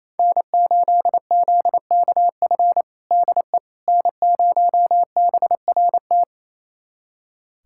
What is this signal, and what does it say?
Morse code, heard as a keyed tone, about 27 wpm, 700 Hz. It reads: N8ZKF DE N0BRT
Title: USA Callsign DE Callsign - 5 Characters - 28wpm